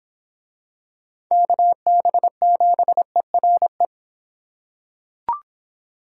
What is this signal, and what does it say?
Morse code, heard as a keyed tone, about 24 wpm, 700 Hz. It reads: KB7ERE